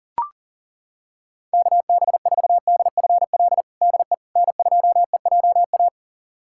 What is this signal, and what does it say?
Morse code, heard as a keyed tone, about 40 wpm, 700 Hz. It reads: KB4DFL DE N2EJA
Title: USA Callsign DE Callsign 40wpm